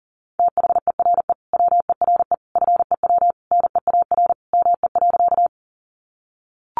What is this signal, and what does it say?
Morse code, heard as a keyed tone, about 40 wpm, 700 Hz. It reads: THERE WERE FEW NEAR ME.
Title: Sentences from Top 300 Words 40wpm